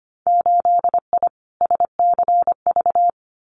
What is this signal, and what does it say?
Morse code, heard as a keyed tone, about 25 wpm, 700 Hz. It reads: 8I SCV
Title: Winter Field Day Exchanges Encore 25wpm